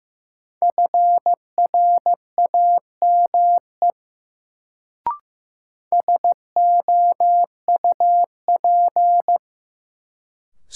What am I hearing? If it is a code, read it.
Morse code, heard as a keyed tone, about 15 wpm, 700 Hz. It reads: FRAME SOUP
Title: Sentences from Top 1000 Words - Review Part 6 - New Word Drill - 15wpm